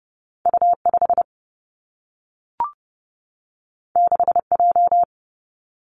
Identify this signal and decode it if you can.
Morse code, heard as a keyed tone, about 30 wpm, 700 Hz. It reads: U 5 6 J
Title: Two Challenging Letter-Number Combinations 30wpm